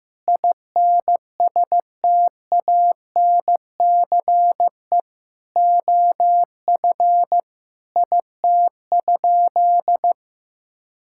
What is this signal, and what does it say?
Morse code, heard as a keyed tone, about 15 wpm, 700 Hz. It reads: INSTANCE OF IT?
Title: Sentences from Top 1000 Words - Review Part 5 - 15wpm